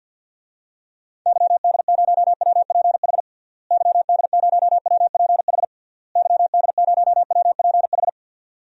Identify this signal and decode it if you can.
Morse code, heard as a keyed tone, about 53 wpm, 700 Hz. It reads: YD0WPH YD0WPH YD0WPH
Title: DXCC-100 Call Signs - 6 Characters - Repeated 3x - 50wpm